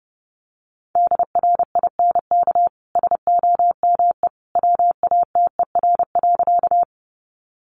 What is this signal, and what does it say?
Morse code, heard as a keyed tone, about 30 wpm, 700 Hz. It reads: DRINK SOME WATER.
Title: Sentences from Top 1500 Words 30wpm